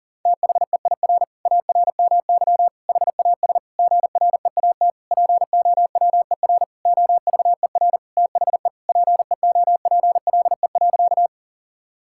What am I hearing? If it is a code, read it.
Morse code, heard as a keyed tone, about 37 wpm, 700 Hz. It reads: THEIR ARMY HAS GREAT POWER OVER THE PEOPLE.
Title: Sentences from Top 1000 Words - Review Part 5 - 40wpm